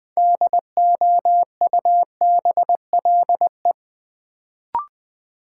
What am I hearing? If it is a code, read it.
Morse code, heard as a keyed tone, about 20 wpm, 700 Hz. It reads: DOUBLE
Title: Sentences from Top 1400 Words - New Word Drill Alt - 20wpm